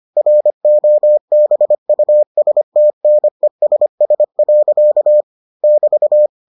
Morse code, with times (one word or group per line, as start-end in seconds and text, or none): 0.2-5.2 ROBUSTNESS.
5.6-6.3 <BT>